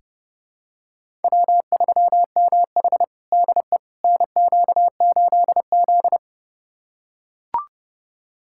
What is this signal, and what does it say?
Morse code, heard as a keyed tone, about 30 wpm, 700 Hz. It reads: W3MH DE NQ8Z
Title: USA Callsign DE Callsign - 4 Characters - 30wpm